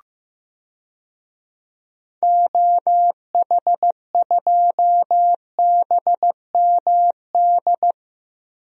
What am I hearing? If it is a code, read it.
Morse code, heard as a keyed tone, about 15 wpm, 700 Hz. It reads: OH2BMD
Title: DXCC-100 Call Signs - 6 Characters - Repeated 3x - 15wpm